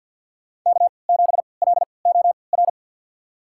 Morse code, heard as a keyed tone, about 53 wpm, spaced as for 29 wpm, 700 Hz. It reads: K7LOR
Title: USA Call Signs - 5 Characters - 3x Character Spacing - 50wpm